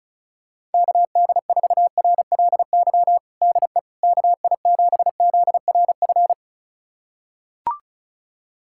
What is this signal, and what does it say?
Morse code, heard as a keyed tone, about 36 wpm, 700 Hz. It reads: KD4RLY DE KI7ZRF
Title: USA Callsign DE Callsign 35wpm